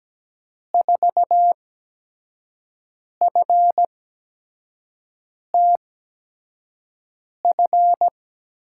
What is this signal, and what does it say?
Morse code, heard as a keyed tone, about 17 wpm, spaced as for 5 wpm, 700 Hz. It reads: 4FTF
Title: USA Call Signs - 6 Characters - 8x Character Spacing - 17wpm